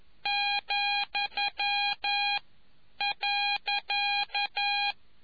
Morse code, heard as a keyed tone, about 11 wpm, 2450 Hz, with noise. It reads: , .